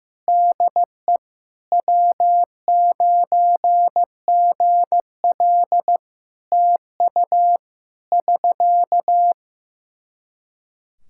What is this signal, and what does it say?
Morse code, heard as a keyed tone, about 15 wpm, 700 Hz. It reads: DE W9GL TU <SK>
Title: K1USN SST QSO Practice 15wpm